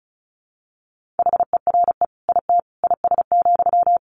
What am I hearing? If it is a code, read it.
Morse code, heard as a keyed tone, about 34 wpm, 700 Hz. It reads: HERE IT IS,
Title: Sentences from Top 1500 Words 35wpm